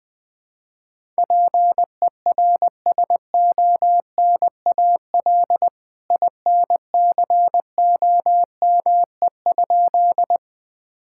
Morse code, heard as a keyed tone, about 20 wpm, 700 Hz. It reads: PERSONAL INCOME?
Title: Sentences from Top 1000 Words - Review Part 5 - 20wpm